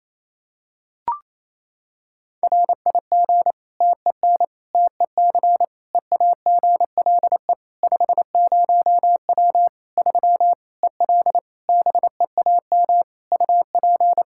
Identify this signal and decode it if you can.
Morse code, heard as a keyed tone, about 28 wpm, 700 Hz. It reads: RIG TEN TEC EAGLE 50W 3 EL BEAM UP
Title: Practice QSO - Exchange 2 - Info - 28wpm